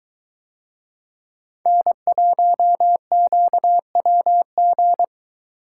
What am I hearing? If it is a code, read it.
Morse code, heard as a keyed tone, about 22 wpm, 700 Hz. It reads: N1QWG